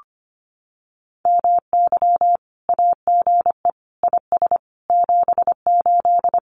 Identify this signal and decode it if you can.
Morse code, heard as a keyed tone, about 25 wpm, 700 Hz. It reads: MY AGE IS 78
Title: Common QSO Phrases Encore 25wpm